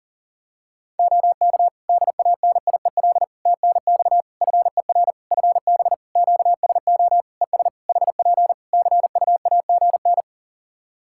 Morse code, heard as a keyed tone, about 40 wpm, 700 Hz. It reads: OK DANIEL TNX FER FB QSO ES HP CUAGN
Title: K1USN SST QSO Practice 40wpm